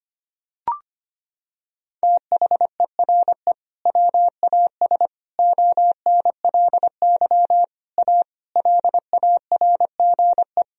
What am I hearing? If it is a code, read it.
Morse code, heard as a keyed tone, about 25 wpm, 700 Hz. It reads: THERE WAS ONLY A LARGE
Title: Sentences from Top 1000 Words - Review Part 2 - 25wpm